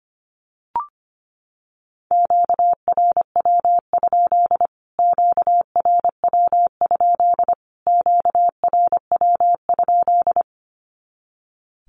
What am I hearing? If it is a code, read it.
Morse code, heard as a keyed tone, about 25 wpm, 700 Hz. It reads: QRW? QRW? QRW?
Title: Q-Codes Repeated 3x 25wpm